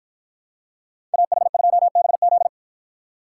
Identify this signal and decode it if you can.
Morse code, heard as a keyed tone, about 49 wpm, 700 Hz. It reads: AH2BZ